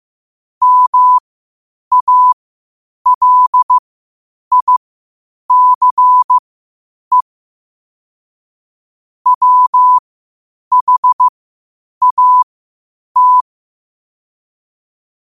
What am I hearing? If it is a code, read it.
Morse code, heard as a keyed tone, about 15 wpm, spaced as for 8 wpm, 1000 Hz. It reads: MALICE WHAT
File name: 8wpm